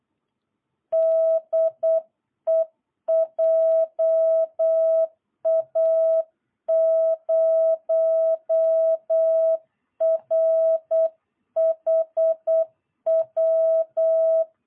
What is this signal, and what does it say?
Morse code, heard as a keyed tone, about 8 wpm, 650 Hz. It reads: DE JA0RHW
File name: jh8cbh / 2022cw